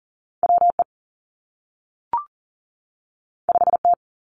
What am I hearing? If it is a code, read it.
Morse code, heard as a keyed tone, about 40 wpm, 700 Hz. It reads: W E 5 T
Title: Two Challenging Letter-Number Combinations 40wpm